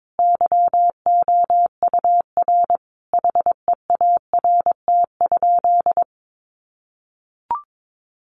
Morse code, heard as a keyed tone, about 22 wpm, 700 Hz. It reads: YOUR HEART?
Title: Sentences from Top 1400 Words 22wpm